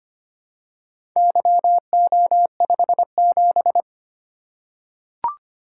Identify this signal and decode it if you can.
Morse code, heard as a keyed tone, about 25 wpm, 700 Hz. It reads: YO57
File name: Four Letter-Number Combinations - 25wpm